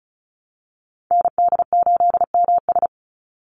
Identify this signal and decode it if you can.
Morse code, heard as a keyed tone, about 34 wpm, 700 Hz. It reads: ND8MS
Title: Call Signs Repeated 2x - 35wpm